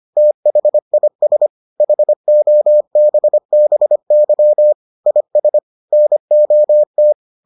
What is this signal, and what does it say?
Morse code, heard as a keyed tone, about 25 wpm, 600 Hz. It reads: THIS HOBBY IS NOT